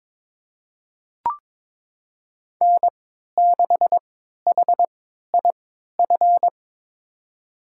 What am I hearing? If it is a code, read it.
Morse code, heard as a keyed tone, about 21 wpm, spaced as for 12 wpm, 700 Hz. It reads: N6HIF
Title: USA Call Signs - 3x Character Spacing - 22wpm